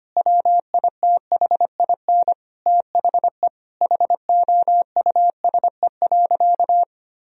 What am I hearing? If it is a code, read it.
Morse code, heard as a keyed tone, about 25 wpm, 700 Hz. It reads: WITHIN THE HOUSE.
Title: Sentences from Top 1000 Words - Review Part 4 - 25wpm